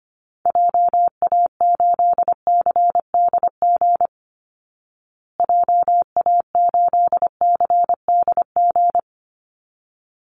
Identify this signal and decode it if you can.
Morse code, heard as a keyed tone, about 25 wpm, 700 Hz. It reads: JA8CDG JA8CDG
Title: Call Signs Repeated 3x - 25wpm